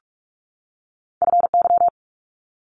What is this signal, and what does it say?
Morse code, heard as a keyed tone, about 46 wpm, 700 Hz. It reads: FY
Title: Multi-letter Phonemes 45wpm